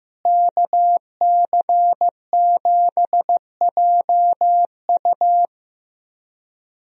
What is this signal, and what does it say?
Morse code, heard as a keyed tone, about 15 wpm, 700 Hz. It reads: KC7JU
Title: USA Call Signs - 5 Characters - Repeated 2x - 15wpm